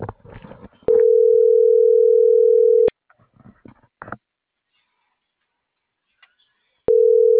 Ringback tone (North America), 440 + 480 Hz.